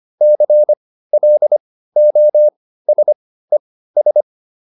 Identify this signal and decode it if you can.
Morse code, heard as a keyed tone, about 25 wpm, spaced as for 15 wpm, 600 Hz. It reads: CLOSES
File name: ES6 15